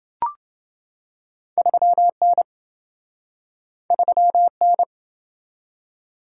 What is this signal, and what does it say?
Morse code, heard as a keyed tone, about 26 wpm, 700 Hz. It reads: 3N 3N